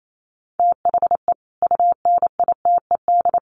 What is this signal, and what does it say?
Morse code, heard as a keyed tone, about 28 wpm, 700 Hz. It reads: THE UNITED
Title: Sets of 3 Words 28wpm